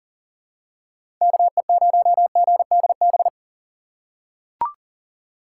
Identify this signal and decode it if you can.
Morse code, heard as a keyed tone, about 40 wpm, 700 Hz. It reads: KE0GDB
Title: USA Call Signs - 4x Character Spacing - 40wpm